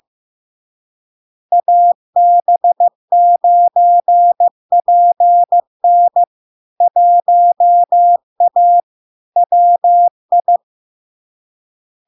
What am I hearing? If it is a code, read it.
Morse code, heard as a keyed tone, about 15 wpm, 700 Hz. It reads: AB9PN 1A WI
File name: LARK-FD-2022-Log-in-CW-15wpm-700-Hz